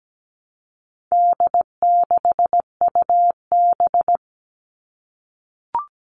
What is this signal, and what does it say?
Morse code, heard as a keyed tone, about 17 wpm, 700 Hz. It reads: D6UB
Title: All Cut Numbers - 4-Digit 17wpm